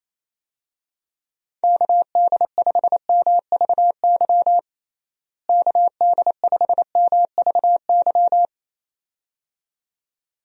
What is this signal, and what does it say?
Morse code, heard as a keyed tone, about 28 wpm, 700 Hz. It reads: KD5MVY KD5MVY
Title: USA Call Signs - 6 Characters - Repeated 2x - 28wpm